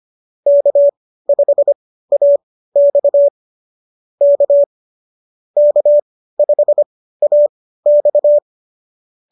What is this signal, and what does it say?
Morse code, heard as a keyed tone, about 25 wpm, spaced as for 15 wpm, 590 Hz. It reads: K5AX K K5AX